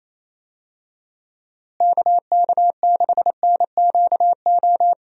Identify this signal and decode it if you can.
Morse code, heard as a keyed tone, about 28 wpm, 700 Hz. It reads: KK6NQO
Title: USA Call Signs - 6 Characters - Repeated 2x - 28wpm